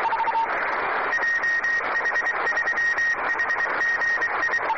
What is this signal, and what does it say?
Morse code, heard as a keyed tone, about 23 wpm, 1850 Hz. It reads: JH3HGI